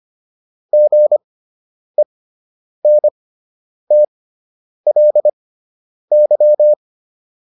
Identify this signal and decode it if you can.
Morse code, heard as a keyed tone, about 25 wpm, spaced as for 9 wpm, 600 Hz. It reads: GENTLY